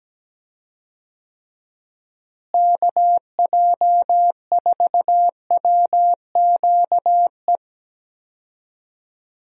Morse code, heard as a keyed tone, about 17 wpm, 700 Hz. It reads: KJ4WQE
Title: USA Call Signs - 6 Characters - Repeated 2x - 17wpm